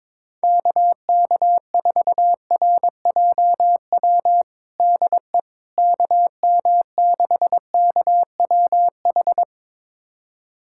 Morse code, heard as a keyed tone, about 22 wpm, 700 Hz. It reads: KK4RJW DE KM6KWH
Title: USA Callsign DE Callsign 22wpm